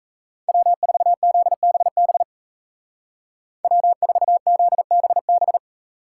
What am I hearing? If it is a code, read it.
Morse code, heard as a keyed tone, about 34 wpm, 700 Hz. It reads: W4ZBB W4ZBB
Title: Speed Racing - USA 1x3 Call Signs 28wpm